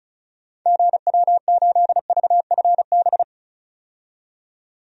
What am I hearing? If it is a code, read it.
Morse code, heard as a keyed tone, about 34 wpm, 700 Hz. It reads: GW8VFB
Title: DXCC-100 Call Signs - 6 Characters - 35wpm